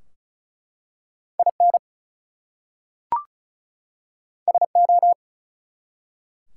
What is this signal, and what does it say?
Morse code, heard as a keyed tone, about 34 wpm, 700 Hz. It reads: IN SO
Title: Two Letter Words - 35wpm